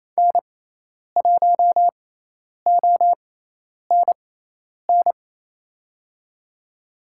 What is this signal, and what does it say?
Morse code, heard as a keyed tone, about 28 wpm, spaced as for 10 wpm, 700 Hz. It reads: N1ONN
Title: USA Call Signs - 5 Characters - 6x Character Spacing - 28wpm